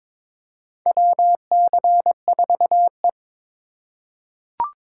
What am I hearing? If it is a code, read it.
Morse code, heard as a keyed tone, about 22 wpm, 700 Hz. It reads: WC4E
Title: USA Call Signs - 4 Characters - 4x Character Spacing - 22wpm